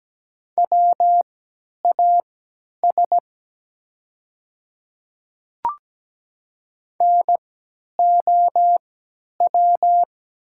Morse code, heard as a keyed tone, about 17 wpm, spaced as for 10 wpm, 700 Hz. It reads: WAS NOW
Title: 3x Character Spacing - Top 100 Words - 17wpm